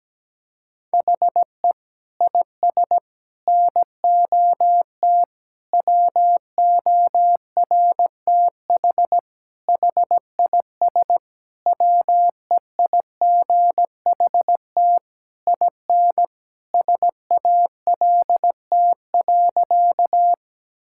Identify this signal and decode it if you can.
Morse code, heard as a keyed tone, about 17 wpm, 700 Hz. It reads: HE IS NOT WORTH HIS WEIGHT IN SALT.
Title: Sentences from Top 1000 Words - Review Part 5 - 17wpm